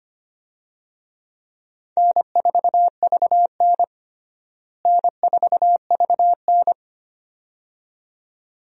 Morse code, heard as a keyed tone, about 25 wpm, 700 Hz. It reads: N4VN N4VN
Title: USA Call Signs - 4 Characters - Repeated 2x - 25wpm